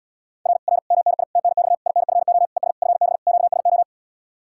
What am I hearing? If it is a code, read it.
Morse code, heard as a keyed tone, about 80 wpm, 700 Hz. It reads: R R MNI TNX INFO ES FB QSO